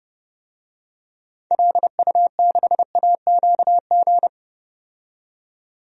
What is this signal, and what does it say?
Morse code, heard as a keyed tone, about 31 wpm, 700 Hz. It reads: LU6AQG
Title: DXCC-100 Call Signs - 6 Characters - 30wpm